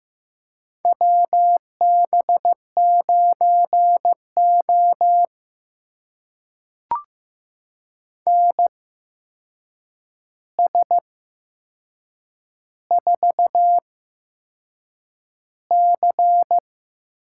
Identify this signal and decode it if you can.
Morse code, heard as a keyed tone, about 15 wpm, spaced as for 6 wpm, 700 Hz. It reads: WB9O NS4C
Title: USA Call Signs - 4 Characters - 8x Character Spacing - 15wpm